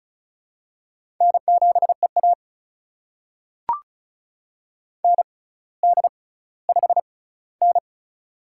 Morse code, heard as a keyed tone, about 36 wpm, spaced as for 18 wpm, 700 Hz. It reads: N7EA ND5N
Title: USA Call Signs - 4 Characters - 6x Character Spacing - 35wpm